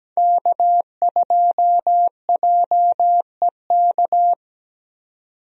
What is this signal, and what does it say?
Morse code, heard as a keyed tone, about 17 wpm, 700 Hz. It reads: K2JEK
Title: USA Call Signs - 5 Characters - 17wpm